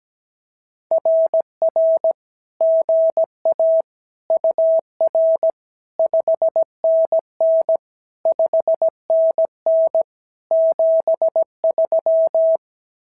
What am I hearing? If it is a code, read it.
Morse code, heard as a keyed tone, about 16 wpm, 650 Hz. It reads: RR GA UR 5NN 5NN 73
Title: SOTA QSO Practice 17wpm